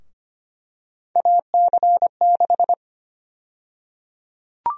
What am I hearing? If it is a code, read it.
Morse code, heard as a keyed tone, about 25 wpm, 700 Hz. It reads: AC6
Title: USA Call Sign Prefixes - Repeated 2x - 25wpm